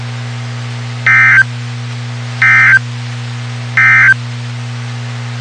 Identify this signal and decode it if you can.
Emergency Alert System (SAME) tones, heard as harsh buzzing bursts. It is heard from 0.9 to 4.2 s.
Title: WXK91 NOAA Weather Radio